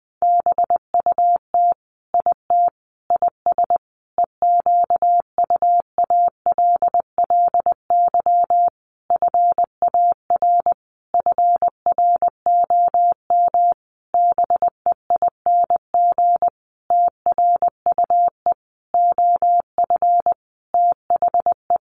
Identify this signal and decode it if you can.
Morse code, heard as a keyed tone, about 20 wpm, 700 Hz. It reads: BUT IT IS EQUALLY FAR FROM BEING TRUE OF THE
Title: Sentences from Top 1400 Words 20wpm